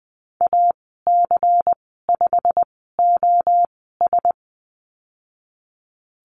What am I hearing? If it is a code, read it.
Morse code, heard as a keyed tone, about 20 wpm, spaced as for 14 wpm, 700 Hz. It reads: AC5OS
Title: 154 Lesson 25g - Letter p - Call Signs 20wpm